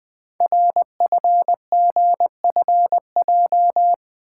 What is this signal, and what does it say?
Morse code, heard as a keyed tone, about 20 wpm, 700 Hz. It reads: RFGFJ
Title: Code Groups - 5 Letters - 20wpm